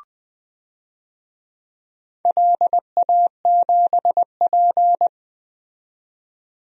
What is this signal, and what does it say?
Morse code, heard as a keyed tone, about 20 wpm, 700 Hz. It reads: LA7P
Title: DXCC-100 Call Signs - 4 Characters - 20wpm